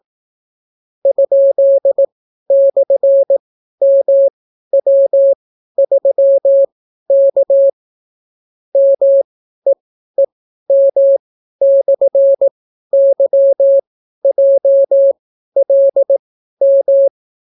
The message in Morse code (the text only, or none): ?/MW3K MEEM/YJLM